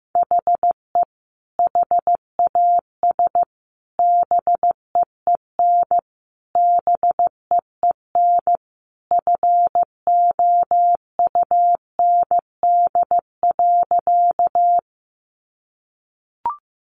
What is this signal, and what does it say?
Morse code, heard as a keyed tone, about 15 wpm, 700 Hz. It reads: HE HAS BEEN BEEN FOUND.
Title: Sentences from Top 300 Words 15wpm